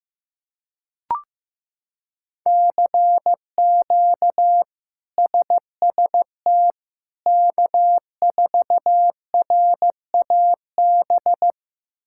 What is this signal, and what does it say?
Morse code, heard as a keyed tone, about 16 wpm, 700 Hz. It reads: CQ SST K4RAB
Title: K1USN SST QSO Practice - 15wpm